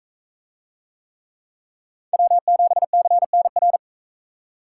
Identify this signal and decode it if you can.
Morse code, heard as a keyed tone, about 38 wpm, 700 Hz. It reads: W7CNR